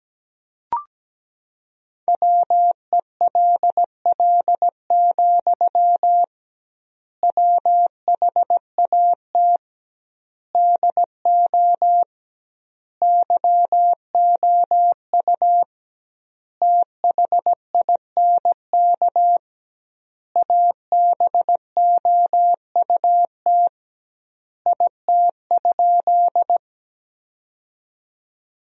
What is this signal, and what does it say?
Morse code, heard as a keyed tone, about 17 wpm, 700 Hz. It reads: WELL, WHAT DO YOU THINK ABOUT IT?
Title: 2x Word-Spacing - Sentences from Top 200 Words - 17wpm